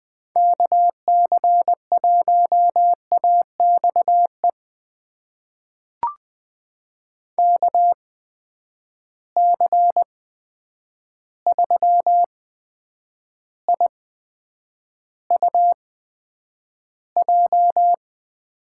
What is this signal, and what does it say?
Morse code, heard as a keyed tone, about 20 wpm, spaced as for 9 wpm, 700 Hz. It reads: KC1AXE KC3IUJ
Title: USA Call Signs - 6 Characters - 8x Character Spacing - 20wpm